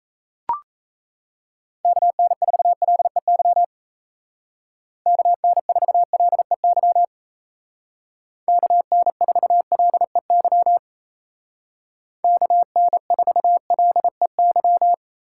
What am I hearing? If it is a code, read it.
Morse code, heard as a keyed tone, about 36 wpm, 700 Hz. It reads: KN4LEY KN4LEY KN4LEY KN4LEY